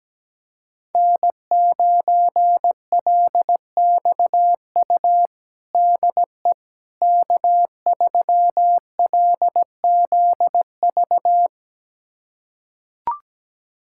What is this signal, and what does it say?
Morse code, heard as a keyed tone, about 17 wpm, 700 Hz. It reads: N9LXU DE K3LZV
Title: USA Callsign DE Callsign - 5 Characters - 17wpm